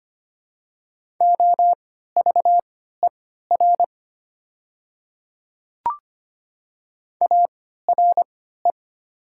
Morse code, heard as a keyed tone, about 25 wpm, spaced as for 14 wpm, 700 Hz. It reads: OVER ARE
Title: 3x Character Spacing - Top 100 Words - 25wpm